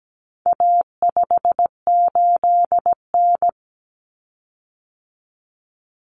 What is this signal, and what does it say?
Morse code, heard as a keyed tone, about 17 wpm, 700 Hz. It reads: A58N
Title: Common Cut Numbers - 4-Digit 17wpm